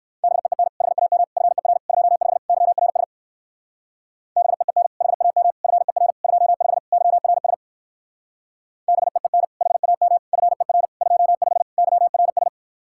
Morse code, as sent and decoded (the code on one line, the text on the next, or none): -... . -. / .... .- -- / ..-. .-. / ..--- ..... / -.-- .-. ... / -... . -. / .... .- -- / ..-. .-. / ..--- ..... / -.-- .-. ... / -... . -. / .... .- -- / ..-. .-. / ..--- ..... / -.-- .-. ...
BEEN HAM FER 25 YRS BEEN HAM FER 25 YRS BEEN HAM FER 25 YRS